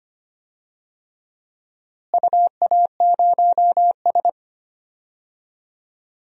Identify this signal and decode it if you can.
Morse code, heard as a keyed tone, about 25 wpm, 700 Hz. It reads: UA0S
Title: DXCC-100 Call Signs - 4 Characters - Repeated 3x - 25wpm